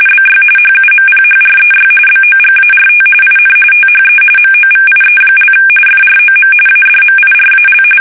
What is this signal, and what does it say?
RTTY, about 45.45 baud. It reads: BY HUMPHREY BOGART IN THE MOVIE) FROM ADMIRAL
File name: fsk850 / battleshipnj